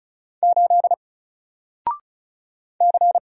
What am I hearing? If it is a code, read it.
Morse code, heard as a keyed tone, about 36 wpm, 700 Hz. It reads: 8 C
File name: Single-Letter-Number-35wpm